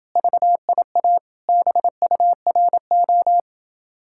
Morse code, heard as a keyed tone, about 28 wpm, 700 Hz. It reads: VIA BURO